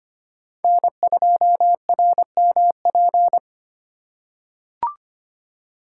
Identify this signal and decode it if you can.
Morse code, heard as a keyed tone, about 25 wpm, 700 Hz. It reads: N2RMP
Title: USA Call Signs - 6x Character Spacing - 25wpm